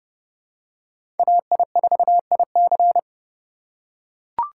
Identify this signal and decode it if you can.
Morse code, heard as a keyed tone, about 31 wpm, 700 Hz. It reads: AI4IC
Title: USA Call Signs - 5 Characters - 2x Character Spacing - 30wpm